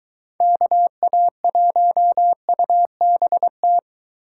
Morse code, heard as a keyed tone, about 23 wpm, 700 Hz. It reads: KA1UBT